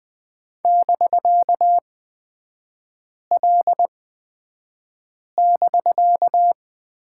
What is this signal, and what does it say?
Morse code, heard as a keyed tone, about 20 wpm, spaced as for 5 wpm, 700 Hz. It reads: <BK>L<BK>
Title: 250 Lesson 40e - BK - Three Characters 20wpm